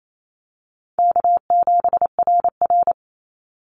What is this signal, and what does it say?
Morse code, heard as a keyed tone, about 28 wpm, 700 Hz. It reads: K7RR
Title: Easy Call Signs - 28wpm